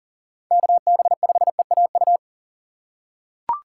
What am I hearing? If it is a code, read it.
Morse code, heard as a keyed tone, about 37 wpm, 700 Hz. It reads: KB5EAU
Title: USA Call Signs - 6 Characters - 4x Character Spacing - 40wpm